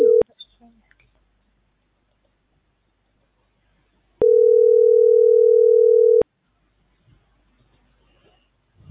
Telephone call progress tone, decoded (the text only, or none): ringback tone